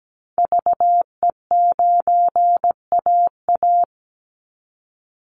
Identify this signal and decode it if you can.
Morse code, heard as a keyed tone, about 17 wpm, 700 Hz. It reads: VE9AA
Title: Call Signs Repeated 3x - 17wpm